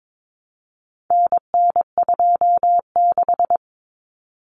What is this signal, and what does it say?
Morse code, heard as a keyed tone, about 22 wpm, 700 Hz. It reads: NN26
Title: Common Cut Numbers - 4-Digit 22wpm